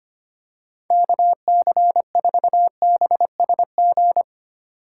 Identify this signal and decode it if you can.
Morse code, heard as a keyed tone, about 25 wpm, 700 Hz. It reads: KC4BSG
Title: USA Call Signs - 6 Characters - 2x Character Spacing - 25wpm